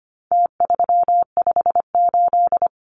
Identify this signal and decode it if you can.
Morse code, heard as a keyed tone, about 25 wpm, 700 Hz. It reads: T358
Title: Numbers - Contest Style - 4-Digit 25wpm